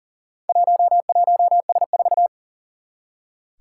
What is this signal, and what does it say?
Morse code, heard as a keyed tone, about 38 wpm, 700 Hz. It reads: 11S4
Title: Four Challenging Letter-Number Combinations 40wpm